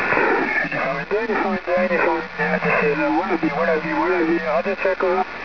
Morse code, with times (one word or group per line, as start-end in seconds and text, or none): none